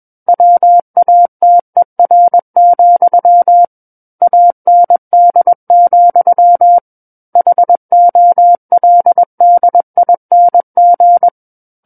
Morse code, heard as a keyed tone, about 21 wpm, 700 Hz. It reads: WATER, AND, HOLDING
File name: Book One / 12-Chapter 12 - 21 WPM